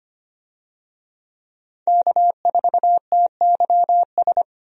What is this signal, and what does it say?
Morse code, heard as a keyed tone, about 25 wpm, 700 Hz. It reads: K4TYS
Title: USA Call Signs - 5 Characters - Repeated 2x - 25wpm